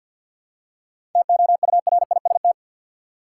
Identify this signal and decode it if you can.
Morse code, heard as a keyed tone, about 53 wpm, 700 Hz. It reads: TOURIST
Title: Seven Letter Words - 50wpm